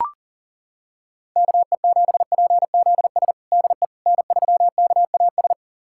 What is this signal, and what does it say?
Morse code, heard as a keyed tone, about 37 wpm, 700 Hz. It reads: KE7PZS DE N3KAS
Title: USA Callsign DE Callsign 40wpm